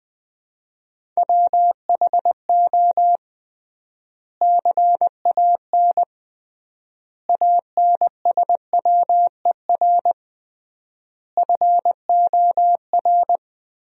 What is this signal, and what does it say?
Morse code, heard as a keyed tone, about 20 wpm, 700 Hz. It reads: WHO CAN ANSWER FOR
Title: 3x Word-Spacing - Sentences from Top 300 Words - 20wpm